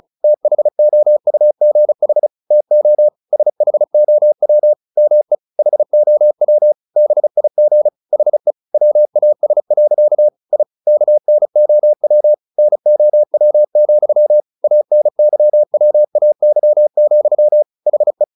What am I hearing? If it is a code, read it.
Morse code, heard as a keyed tone, about 34 wpm, 600 Hz. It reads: THOUGH TO SHOW ME HOW BIG HE WAS. I KNOW NOW, ANYWAY, HE
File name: the old man and the sea / the old man and the sea 1 35wpm